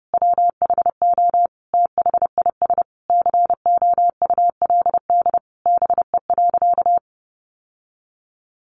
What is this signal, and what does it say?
Morse code, heard as a keyed tone, about 30 wpm, 700 Hz. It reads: WHO THIS COULD BE.
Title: Sentences from Top 1400 Words 30wpm